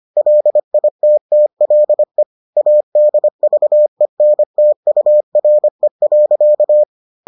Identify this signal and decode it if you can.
Morse code, heard as a keyed tone, about 25 wpm, 600 Hz. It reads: LITTLE ADVENTURE.